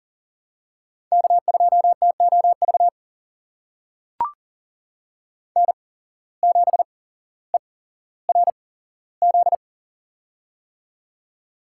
Morse code, heard as a keyed tone, about 38 wpm, spaced as for 17 wpm, 700 Hz. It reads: K2TOV N7ERZ
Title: USA Call Signs - 5 Characters - 8x Character Spacing - 40wpm